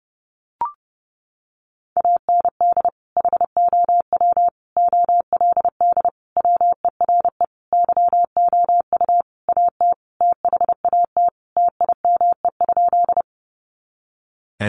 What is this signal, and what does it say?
Morse code, heard as a keyed tone, about 28 wpm, 700 Hz. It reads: AND HOW OLD WERE YOU AT THAT TIME?
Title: Sentences from Top 200 Words 30wpm